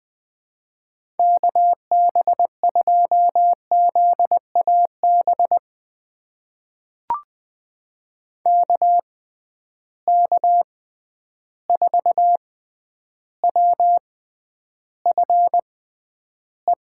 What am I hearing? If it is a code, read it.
Morse code, heard as a keyed tone, about 20 wpm, spaced as for 10 wpm, 700 Hz. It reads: KB2ZAB KK4WFE
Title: USA Call Signs - 6 Characters - 6x Character Spacing - 20wpm